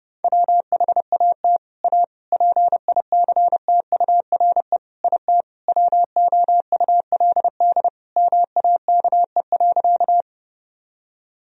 Morse code, heard as a keyed tone, about 32 wpm, 700 Hz. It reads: WHAT A PICTURE IT WOULD MAKE.
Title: Sentences from Top 1000 Words - Review Part 1 - 30wpm